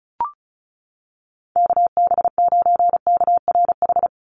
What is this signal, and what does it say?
Morse code, heard as a keyed tone, about 34 wpm, 700 Hz. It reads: KB9KRH
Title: USA Call Signs 35wpm